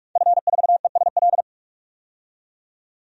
Morse code, heard as a keyed tone, about 47 wpm, 700 Hz. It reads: F4ESL
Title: DXCC-100 Call Signs - 5 Characters - 45wpm